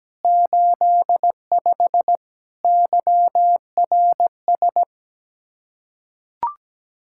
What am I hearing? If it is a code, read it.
Morse code, heard as a keyed tone, about 17 wpm, 700 Hz. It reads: 85 YRS
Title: Practice QSO - Exchange 3 - More Info - 17wpm